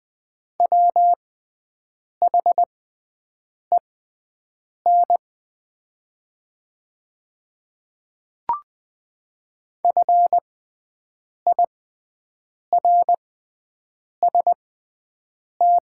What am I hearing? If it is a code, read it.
Morse code, heard as a keyed tone, about 20 wpm, spaced as for 7 wpm, 700 Hz. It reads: WHEN FIRST
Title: 6x Character Spacing - Top 300 Words - 20wpm